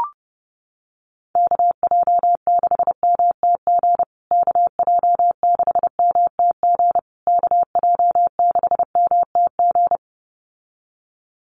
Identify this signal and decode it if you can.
Morse code, heard as a keyed tone, about 28 wpm, 700 Hz. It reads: KJ6MTG KJ6MTG KJ6MTG
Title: USA Call Signs Repeated 3x 30wpm